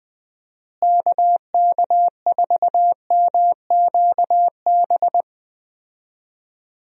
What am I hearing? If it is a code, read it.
Morse code, heard as a keyed tone, about 20 wpm, 700 Hz. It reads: KK4MQB
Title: USA Call Signs - 6 Characters - Repeated 2x - 20wpm